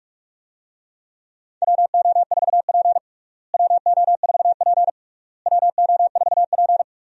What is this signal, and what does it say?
Morse code, heard as a keyed tone, about 44 wpm, 700 Hz. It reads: WO4P WO4P WO4P
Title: USA Call Signs - 4 Characters - Repeated 3x - 45wpm